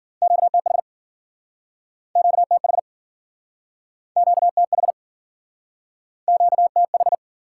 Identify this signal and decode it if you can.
Morse code, heard as a keyed tone, about 56 wpm, 700 Hz. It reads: QTH QTH QTH QTH